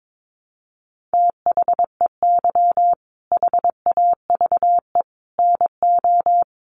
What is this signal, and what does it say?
Morse code, heard as a keyed tone, about 22 wpm, 700 Hz. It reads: THEY HAVE NO
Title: Sentences from Top 1300 Words 22wpm